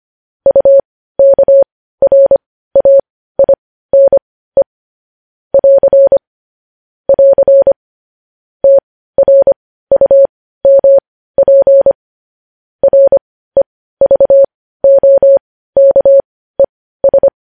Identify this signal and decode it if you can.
Morse code, heard as a keyed tone, about 25 wpm, spaced as for 15 wpm, 550 Hz. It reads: UKRAINE <AR> <AR> TRUMP REVOKES
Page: news headlines in morse code at 15 wpm